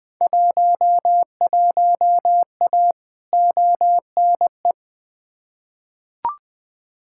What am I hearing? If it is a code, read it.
Morse code, heard as a keyed tone, about 20 wpm, 700 Hz. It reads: 11A ONE
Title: Field Day Exchanges Encore - 20wpm